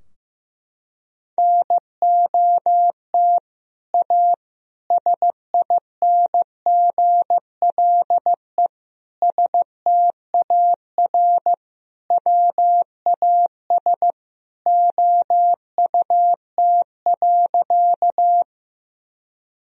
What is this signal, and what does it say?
Morse code, heard as a keyed tone, about 15 wpm, 700 Hz. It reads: NOT A SINGLE STAR WAS OUT.
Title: Sentences from Top 1000 Words - Review Part 4 - 15wpm